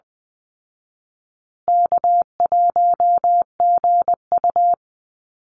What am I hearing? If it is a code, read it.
Morse code, heard as a keyed tone, about 20 wpm, 700 Hz. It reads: K1GU
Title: Call Signs Repeated 2x - 20wpm